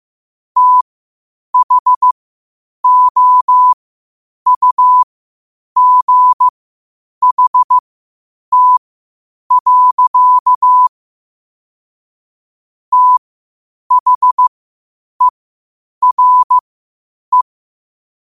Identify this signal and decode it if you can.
Morse code, heard as a keyed tone, about 15 wpm, spaced as for 8 wpm, 1000 Hz. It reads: THOUGHT. THERE
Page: qotd-at-8-wpm